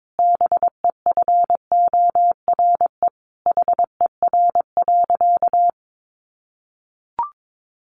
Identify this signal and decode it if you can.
Morse code, heard as a keyed tone, about 22 wpm, 700 Hz. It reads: BEFORE HER.
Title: Sentences from Top 1400 Words 22wpm